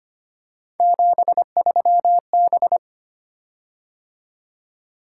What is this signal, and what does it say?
Morse code, heard as a keyed tone, about 25 wpm, 700 Hz. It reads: 73B
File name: Three Letter-Number Combinations - 25wpm